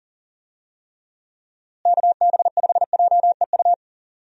Morse code, heard as a keyed tone, about 40 wpm, 700 Hz. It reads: KB5JEU
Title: USA Call Signs - 6 Characters - Repeated 2x - 40wpm